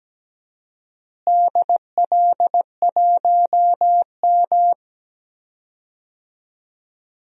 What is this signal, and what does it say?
Morse code, heard as a keyed tone, about 17 wpm, 700 Hz. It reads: DL1M
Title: DXCC-100 Call Signs - 4 Characters - Repeated 2x - 17wpm